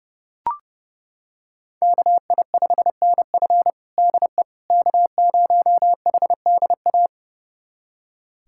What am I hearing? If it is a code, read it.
Morse code, heard as a keyed tone, about 32 wpm, 700 Hz. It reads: KI5NF DE K0HDA
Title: USA Callsign DE Callsign - 5 Characters - 30wpm